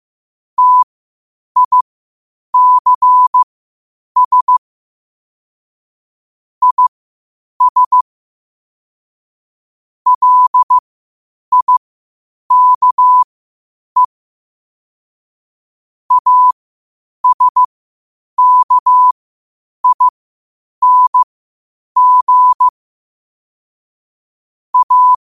Morse code, heard as a keyed tone, about 15 wpm, spaced as for 8 wpm, 1000 Hz. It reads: TICS IS LIKE ASKING A
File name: qotdfiles / 8wpm